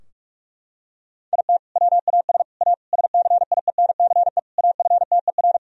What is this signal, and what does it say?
Morse code, heard as a keyed tone, about 46 wpm, 700 Hz. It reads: IT WAS A SCIENCE AFTER
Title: Sentences from Top 1000 Words - Review Part 2 - 45wpm